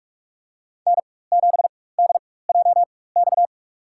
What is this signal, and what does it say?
Morse code, heard as a keyed tone, about 46 wpm, spaced as for 21 wpm, 700 Hz. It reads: N7DJX
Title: USA Call Signs - 5 Characters - 4x Character Spacing - 45wpm